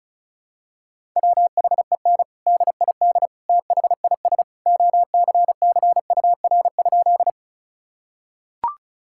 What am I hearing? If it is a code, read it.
Morse code, heard as a keyed tone, about 34 wpm, 700 Hz. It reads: WHEN DID THIS OCCUR?
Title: Sentences from Top 1000 Words - Review Part 6 - 35wpm